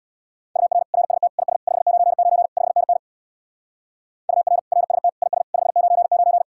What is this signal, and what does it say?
Morse code, heard as a keyed tone, about 59 wpm, 700 Hz. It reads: UR RST IS 599 5NN UR RST IS 599
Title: Speed-Racing Common QSO Phrases Encore - 50wpm